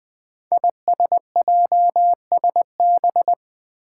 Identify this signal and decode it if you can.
Morse code, heard as a keyed tone, about 20 wpm, 700 Hz. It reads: ISJSB
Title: Five Challenging Letter-Number Combinations 20wpm